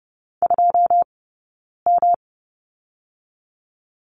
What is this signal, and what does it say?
Morse code, heard as a keyed tone, about 30 wpm, 700 Hz. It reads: 2 M
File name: Lesson 18d - Letter m - Two Characters 30wpm